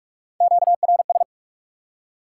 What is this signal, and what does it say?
Morse code, heard as a keyed tone, about 46 wpm, 700 Hz. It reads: QRS
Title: Top 90 QSO Elements 45wpm